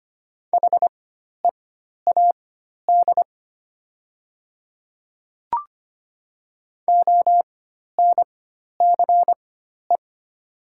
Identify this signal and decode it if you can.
Morse code, heard as a keyed tone, about 25 wpm, spaced as for 12 wpm, 700 Hz. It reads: HEAD ONCE
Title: 4x Character Spacing - Top 300 Words - 25wpm